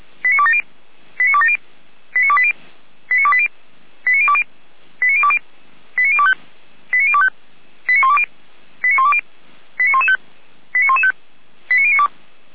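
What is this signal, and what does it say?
Five-tone selective call (ZVEI), heard as quick runs of five tones, tones of about 70 ms. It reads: 89270 89270 89280 89280 89020 89020 89025 89025 89120 89120 89115 89115 89092
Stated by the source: ZVEI five-tone call received on SDR.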